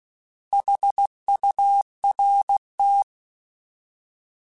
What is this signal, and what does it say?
Morse code, heard as a keyed tone, about 16 wpm, 800 Hz. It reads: HURT
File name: Alphabetical / Alphabetical mp3s / S vs H @15